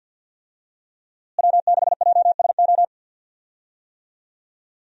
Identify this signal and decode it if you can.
Morse code, heard as a keyed tone, about 51 wpm, 700 Hz. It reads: W6JSO
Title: USA Call Signs - 5 Characters - Repeated 3x - 50wpm